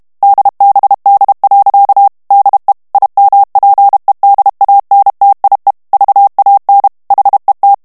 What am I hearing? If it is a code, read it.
Morse code, heard as a keyed tone, about 32 wpm, 800 Hz. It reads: DBD. DE IMPEDANTIE VAN HET
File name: morse mp3 / HOLLANDAIS texte clair 1 vitesse 30 mots minute